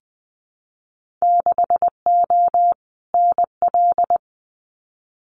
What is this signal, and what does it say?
Morse code, heard as a keyed tone, about 20 wpm, 700 Hz. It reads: 6O NL
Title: Winter Field Day Exchanges Encore 20wpm